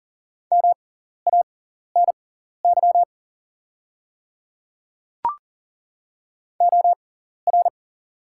Morse code, heard as a keyed tone, about 40 wpm, spaced as for 14 wpm, 690 Hz. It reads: MANY OR
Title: 6x Character Spacing - Top 100 Words - 40wpm